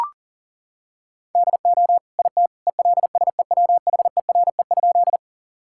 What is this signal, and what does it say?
Morse code, heard as a keyed tone, about 40 wpm, 700 Hz. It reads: DO IT ELSEWHERE?
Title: Sentences from Top 1600 Words 40wpm